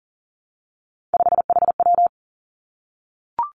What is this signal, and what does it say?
Morse code, heard as a keyed tone, about 38 wpm, 700 Hz. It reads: 5HW
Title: Three Challenging Letter-Number Combinations 40wpm